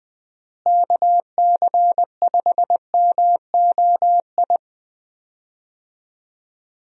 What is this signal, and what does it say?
Morse code, heard as a keyed tone, about 21 wpm, 700 Hz. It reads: KC5MOI
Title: USA Call Signs - 6 Characters - Repeated 3x - 20wpm